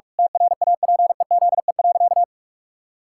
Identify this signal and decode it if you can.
Morse code, heard as a keyed tone, about 46 wpm, 700 Hz. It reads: TRAPEZE.
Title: Pangram Challenge v2 - 45wpm